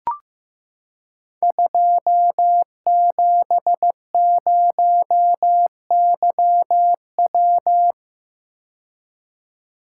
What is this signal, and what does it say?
Morse code, heard as a keyed tone, about 15 wpm, 700 Hz. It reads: 270YW
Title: Code Groups - 5 Letter:Numbers - 15wpm